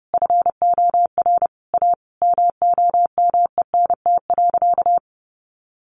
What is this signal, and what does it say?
Morse code, heard as a keyed tone, about 30 wpm, 700 Hz. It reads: FOR A MOMENT.
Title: Sentences from Top 1500 Words 30wpm